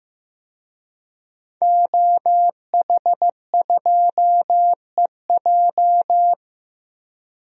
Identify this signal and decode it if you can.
Morse code, heard as a keyed tone, about 15 wpm, 700 Hz. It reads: OH2EJ
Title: DXCC-100 Call Signs - 5 Characters - Repeated 3x - 15wpm